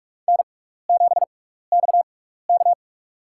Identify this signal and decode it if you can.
Morse code, heard as a keyed tone, about 44 wpm, spaced as for 15 wpm, 700 Hz. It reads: N7XK
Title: USA Call Signs - 4 Characters - 6x Character Spacing - 45wpm